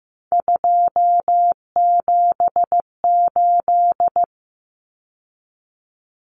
Morse code, heard as a keyed tone, about 15 wpm, 700 Hz. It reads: 278
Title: Common Cut Numbers - 3-Digit 15wpm